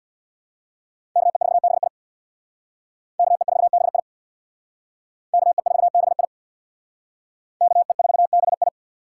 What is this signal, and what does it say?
Morse code, heard as a keyed tone, about 62 wpm, 700 Hz. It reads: KE4BI KE4BI KE4BI KE4BI
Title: Speed Racing - USA Call Signs 50wpm